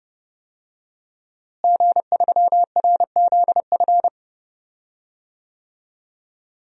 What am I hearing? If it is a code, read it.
Morse code, heard as a keyed tone, about 30 wpm, 700 Hz. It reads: G3RZF
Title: DXCC-100 Call Signs - 5 Characters - Repeated 2x - 30wpm